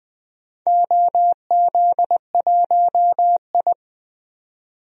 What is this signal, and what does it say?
Morse code, heard as a keyed tone, about 20 wpm, 700 Hz. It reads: OZ1I
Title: DXCC-100 Call Signs - 4 Characters - 20wpm